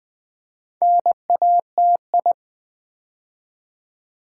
Morse code, heard as a keyed tone, about 20 wpm, 690 Hz. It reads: NATI
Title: Build A Word - 2 Letters At A Time - 6-Letter Words - 20wpm